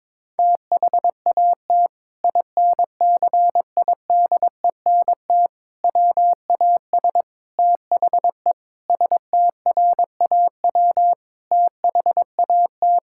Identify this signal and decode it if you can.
Morse code, heard as a keyed tone, about 22 wpm, 700 Hz. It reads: THAT INCIDENT WAS THE STRAW THAT
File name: practice / 22wpm / Sentences with 8-Word Phrases - 22wpm